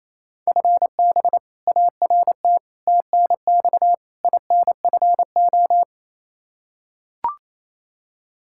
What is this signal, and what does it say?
Morse code, heard as a keyed tone, about 28 wpm, 700 Hz. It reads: FB ART TNX INFO
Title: K1USN SST QSO Practice 28wpm